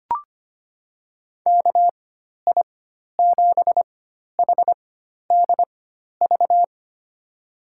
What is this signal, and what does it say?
Morse code, heard as a keyed tone, about 25 wpm, spaced as for 12 wpm, 700 Hz. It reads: KI7HDV
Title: USA Call Signs - 6 Characters - 4x Character Spacing - 25wpm